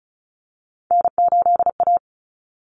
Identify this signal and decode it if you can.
Morse code, heard as a keyed tone, about 34 wpm, 700 Hz. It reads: N8A
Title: Common Cut Numbers - 3-Digit 35wpm